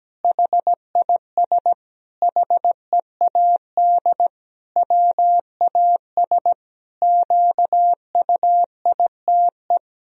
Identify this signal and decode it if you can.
Morse code, heard as a keyed tone, about 17 wpm, 700 Hz. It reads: HIS HEAD WAS QUITE
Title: Sentences from Top 1000 Words - Review Part 2 - 17wpm